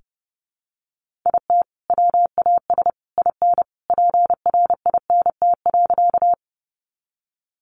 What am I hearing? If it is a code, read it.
Morse code, heard as a keyed tone, about 30 wpm, 700 Hz. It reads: IT WAS IN PRINT.
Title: Sentences from Top 1400 Words 30wpm